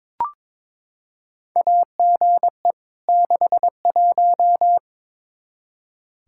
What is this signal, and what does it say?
Morse code, heard as a keyed tone, about 22 wpm, 700 Hz. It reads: AGE 61
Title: Practice QSO - Exchange 3 - More Info - 22wpm